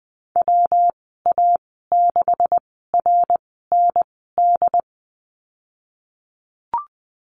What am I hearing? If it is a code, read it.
Morse code, heard as a keyed tone, about 20 wpm, spaced as for 15 wpm, 700 Hz. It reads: WA6RND
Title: 161 Lesson 26g - Letter g - Call Signs 20wpm